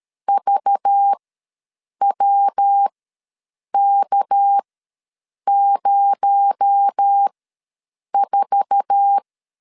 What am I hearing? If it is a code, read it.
Morse code, heard as a keyed tone, about 13 wpm, spaced as for 7 wpm, 800 Hz. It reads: VWK04
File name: LEZIONI / 8 YJ45 / 8 YJ4 ws 12wpm 3249